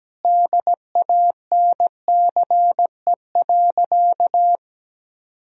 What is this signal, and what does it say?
Morse code, heard as a keyed tone, about 17 wpm, 700 Hz. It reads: DANCE.
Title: Sentences from Top 1000 Words - Review Part 4 - 17wpm